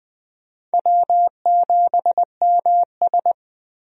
Morse code, heard as a keyed tone, about 20 wpm, 700 Hz. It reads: W7MS
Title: USA Call Signs - 4 Characters - Repeated 3x - 20wpm